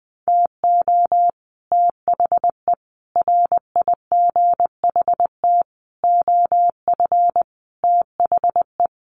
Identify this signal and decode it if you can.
Morse code, heard as a keyed tone, about 20 wpm, 700 Hz. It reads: TO THE RIGHT OF THE
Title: Sets of 5 Words Encore 20wpm